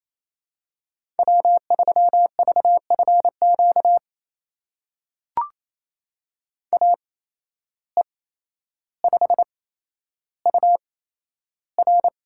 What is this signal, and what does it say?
Morse code, heard as a keyed tone, about 27 wpm, spaced as for 12 wpm, 700 Hz. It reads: W3VFQ AE5UR
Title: USA Call Signs - 5 Characters - 8x Character Spacing - 28wpm